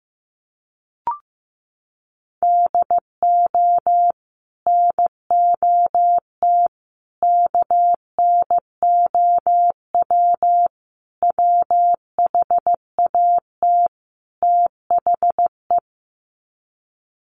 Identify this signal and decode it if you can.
Morse code, heard as a keyed tone, about 15 wpm, 700 Hz. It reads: DO NOT KNOW WHAT THE
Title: Sets of 5 Words 15wpm